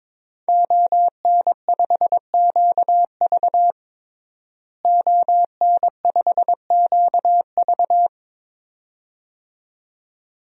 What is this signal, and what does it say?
Morse code, heard as a keyed tone, about 22 wpm, 700 Hz. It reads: ON5QV ON5QV
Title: DXCC-100 Call Signs - 5 Characters - Repeated 2x - 22wpm